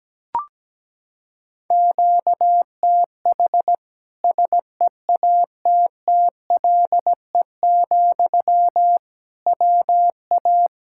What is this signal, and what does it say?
Morse code, heard as a keyed tone, about 17 wpm, 700 Hz. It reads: QTH SEATTLE, WA
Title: Practice QSO - Exchange 1 - Beginning - 17wpm